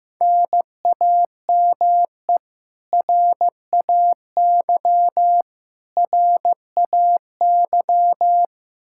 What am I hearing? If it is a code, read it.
Morse code, heard as a keyed tone, about 15 wpm, 700 Hz. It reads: NAME RAY RAY
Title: Practice QSO - Exchange 1 - Beginning - 15wpm